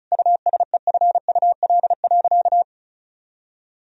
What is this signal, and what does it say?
Morse code, heard as a keyed tone, about 35 wpm, 700 Hz. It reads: USEFUL.
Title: Sentences from Top 1000 Words - Review Part 4 - 35wpm